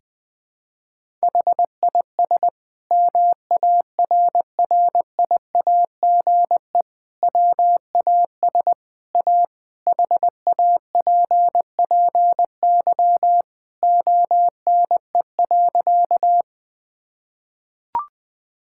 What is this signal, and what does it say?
Morse code, heard as a keyed tone, about 20 wpm, 700 Hz. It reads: HIS MARRIAGE WAS A HAPPY ONE.
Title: Sentences from Top 1000 Words - Review Part 5 - 20wpm